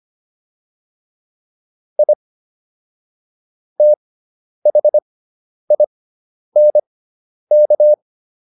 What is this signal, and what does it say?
Morse code, heard as a keyed tone, about 25 wpm, spaced as for 10 wpm, 600 Hz. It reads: I THINK
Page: intermediate-practice-files